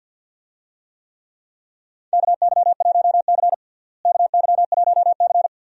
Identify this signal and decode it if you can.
Morse code, heard as a keyed tone, about 51 wpm, 700 Hz. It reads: KY1C KY1C
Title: USA Call Signs - 4 Characters - Repeated 2x - 50wpm